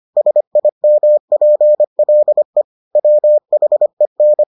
Morse code, heard as a keyed tone, about 25 wpm, 600 Hz. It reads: SIMPLE WHEN